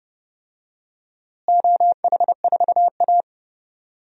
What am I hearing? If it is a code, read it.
Morse code, heard as a keyed tone, about 32 wpm, 700 Hz. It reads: OH4A
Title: DXCC-100 Call Signs - 4 Characters - Repeated 3x - 30wpm